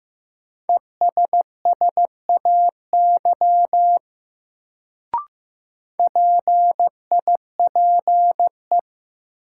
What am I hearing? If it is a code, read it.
Morse code, heard as a keyed tone, about 15 wpm, 700 Hz. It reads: ESSAY PIPE
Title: Sentences from Top 1500 Words - New Word Drill - 15wpm